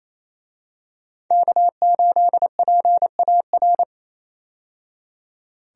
Morse code, heard as a keyed tone, about 28 wpm, 700 Hz. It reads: K8PAR
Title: USA Call Signs - 5 Characters - 28wpm